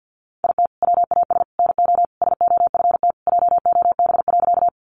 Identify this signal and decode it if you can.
Morse code, heard as a keyed tone, about 50 wpm, 700 Hz. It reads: IT WAS NO SOFT JOB.
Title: Sentences from Top 1500 Words 50wpm